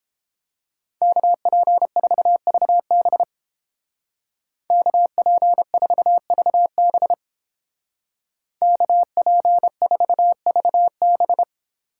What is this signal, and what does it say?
Morse code, heard as a keyed tone, about 31 wpm, 700 Hz. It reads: KP4VB KP4VB KP4VB